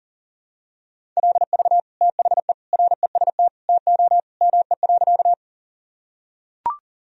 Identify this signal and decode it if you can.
Morse code, heard as a keyed tone, about 40 wpm, 700 Hz. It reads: LV THE REST TO ME.
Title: Sentences from Top 500 Words and Abbreviations 40wpm